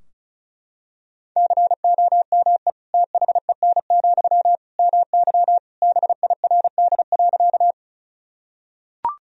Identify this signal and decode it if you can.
Morse code, heard as a keyed tone, about 34 wpm, 700 Hz. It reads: COME THEN, MY BIRD.
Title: Sentences from Top 1000 Words - Review Part 2 - 35wpm